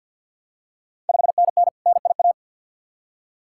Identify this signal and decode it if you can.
Morse code, heard as a keyed tone, about 48 wpm, 690 Hz. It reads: 5NN NIA
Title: Ontario QSO Party Exchange 50wpm